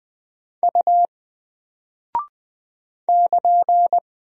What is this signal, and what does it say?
Morse code, heard as a keyed tone, about 20 wpm, 700 Hz. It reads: U <KN>
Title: Single Character (Letter:Number:Punctuation:Prosign) Alt 20wpm